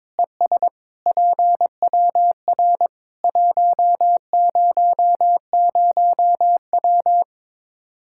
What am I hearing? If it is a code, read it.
Morse code, heard as a keyed tone, about 22 wpm, 700 Hz. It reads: ES PWR 100W
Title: Practice QSO - Exchange 2 - Info - 22wpm